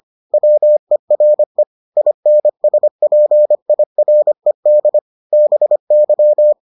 Morse code, heard as a keyed tone, about 25 wpm, 600 Hz. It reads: WERE INSPIRED BY